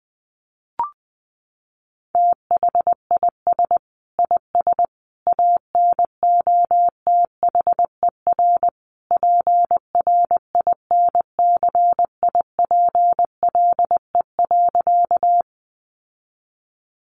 Morse code, heard as a keyed tone, about 20 wpm, 700 Hz. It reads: THIS IS ANOTHER PRINCIPLE.
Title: Sentences from Top 1400 Words 20wpm